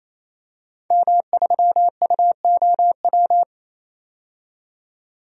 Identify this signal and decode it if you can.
Morse code, heard as a keyed tone, about 28 wpm, 700 Hz. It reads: M3UOW
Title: DXCC-100 Call Signs - 5 Characters - 28wpm